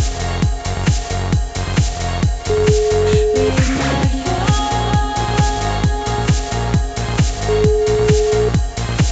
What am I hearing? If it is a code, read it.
Ringback tone (Europe, 425 Hz), with noise.